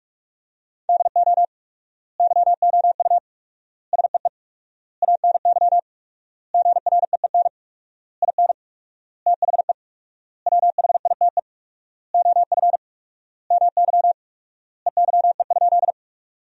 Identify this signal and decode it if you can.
Morse code, heard as a keyed tone, about 46 wpm, 700 Hz. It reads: DO YOU SEE ANY GREEN IN THE WHITE OF MY EYE?
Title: 4x Word-Spacing - Sentences from Top 500 Words - 45wpm